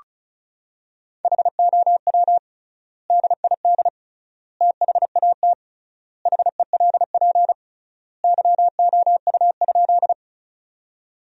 Morse code, heard as a keyed tone, about 34 wpm, 700 Hz. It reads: HOW DID THAT HELP YOU?
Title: 3x Word-Spacing - Sentences from Top 200 Words - 35wpm